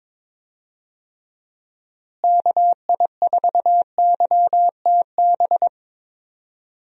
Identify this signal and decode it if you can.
Morse code, heard as a keyed tone, about 22 wpm, 700 Hz. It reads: KI4YTB
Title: USA Call Signs - 6 Characters - Repeated 2x - 22wpm